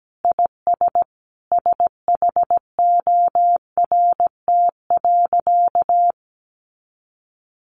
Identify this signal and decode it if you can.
Morse code, heard as a keyed tone, about 17 wpm, 700 Hz. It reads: IS SHORT.
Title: Sentences from Top 500 Words 17wpm